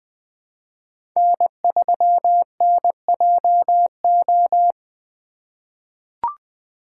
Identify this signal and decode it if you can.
Morse code, heard as a keyed tone, about 21 wpm, 700 Hz. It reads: N3NJO
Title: USA Call Signs - 5 Characters - 3x Character Spacing - 20wpm